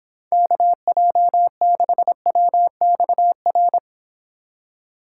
Morse code, heard as a keyed tone, about 29 wpm, 700 Hz. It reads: KJ6WXR